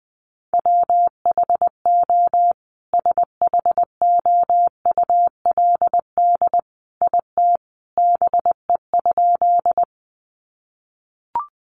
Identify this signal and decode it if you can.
Morse code, heard as a keyed tone, about 20 wpm, 700 Hz. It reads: WHO SHOULD IT BE?
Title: Sentences from Top 300 Words 20wpm